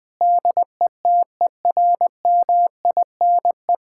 Morse code, heard as a keyed tone, about 21 wpm, 700 Hz. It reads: DETERMINE
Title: Sentences from Top 1000 Words - Review Part 5 - 20wpm